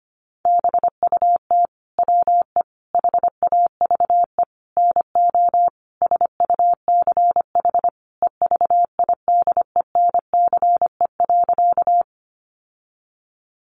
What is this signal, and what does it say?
Morse code, heard as a keyed tone, about 25 wpm, 700 Hz. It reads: BUT WE HAVE NO SUCH EVIDENCE.
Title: Sentences from Top 1300 Words 25wpm